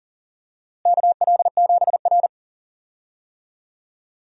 Morse code, heard as a keyed tone, about 40 wpm, 700 Hz. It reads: KL7R
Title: DXCC-100 Call Signs - 4 Characters - Repeated 2x - 40wpm